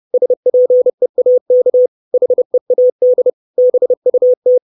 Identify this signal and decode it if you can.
Morse code, heard as a keyed tone, about 30 wpm, 500 Hz. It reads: SPEAK HEAD BUT
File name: 632 Words 30WPM